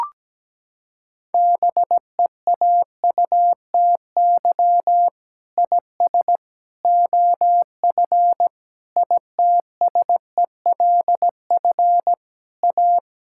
Morse code, heard as a keyed tone, about 17 wpm, 700 Hz. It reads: BEAUTY IS OF ITSELF A
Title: Sentences from Top 1000 Words - Review Part 5 - 17wpm